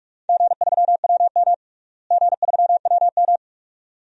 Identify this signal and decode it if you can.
Morse code, heard as a keyed tone, about 42 wpm, 700 Hz. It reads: G3WM G3WM
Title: DXCC-100 Call Signs - 4 Characters - Repeated 2x - 45wpm